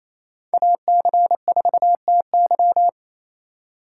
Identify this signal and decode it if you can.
Morse code, heard as a keyed tone, about 28 wpm, 700 Hz. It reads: AC4TY
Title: USA Call Signs - 5 Characters - 2x Character Spacing - 28wpm